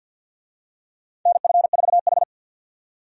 Morse code, heard as a keyed tone, about 52 wpm, 700 Hz. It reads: NF4H
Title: USA Call Signs - 4 Characters - 50wpm